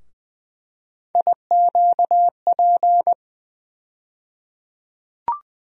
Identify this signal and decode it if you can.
Morse code, heard as a keyed tone, about 20 wpm, 700 Hz. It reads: IQP
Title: USA Call Sign Postfixes - 3 Letters 20wpm